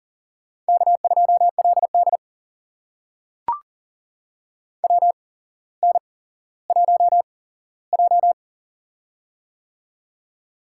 Morse code, heard as a keyed tone, about 40 wpm, spaced as for 17 wpm, 700 Hz. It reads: K2LD WN1J
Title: USA Call Signs - 8x Character Spacing - 40wpm